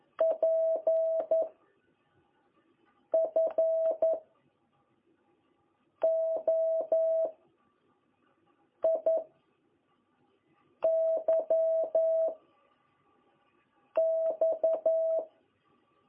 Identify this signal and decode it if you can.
Morse code, heard as a keyed tone, about 11 wpm, 660 Hz. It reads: P F O I Y X
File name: jh8cbh / 2021cw